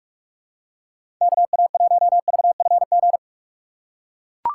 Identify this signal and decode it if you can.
Morse code, heard as a keyed tone, about 44 wpm, 700 Hz. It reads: KA1VFG
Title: USA Call Signs - 6 Characters - 4x Character Spacing - 45wpm